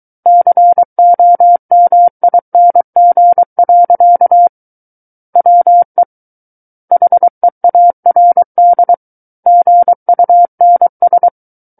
Morse code, heard as a keyed tone, about 23 wpm, 700 Hz. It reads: COMING. WE HEARD GUNS
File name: Book One / 14-Chapter 14 - 23 WPM